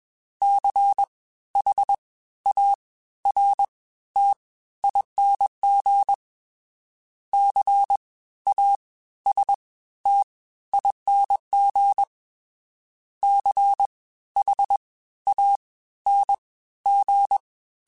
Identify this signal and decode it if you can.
Morse code, heard as a keyed tone, about 21 wpm, 800 Hz. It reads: C H A R T ING C A S T ING C H A N G
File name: Special INGs-mp3s / Special INGs-mp3s / Fast ing @20